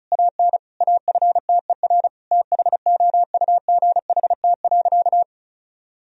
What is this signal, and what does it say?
Morse code, heard as a keyed tone, about 34 wpm, 700 Hz. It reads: AN AFTER THOUGHT.
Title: Sentences from Top 1000 Words - Review Part 2 - 35wpm